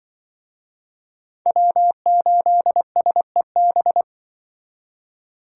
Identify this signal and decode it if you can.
Morse code, heard as a keyed tone, about 24 wpm, 700 Hz. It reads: W8SEB
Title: Speed Racing - USA 1x3 Call Signs 20wpm